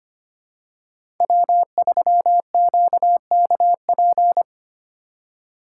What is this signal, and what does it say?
Morse code, heard as a keyed tone, about 25 wpm, 700 Hz. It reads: W3QKP
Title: USA Call Signs - 4x Character Spacing - 25wpm